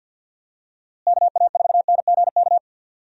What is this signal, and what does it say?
Morse code, heard as a keyed tone, about 46 wpm, 700 Hz. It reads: KA4NGK
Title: USA Call Signs - 6 Characters - Repeated 3x - 50wpm